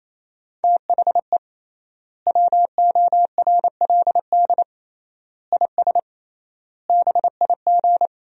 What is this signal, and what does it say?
Morse code, heard as a keyed tone, about 28 wpm, 700 Hz. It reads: THE WORLD IS BIG
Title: 3x Word-Spacing - Sentences from Top 400 Words - 28wpm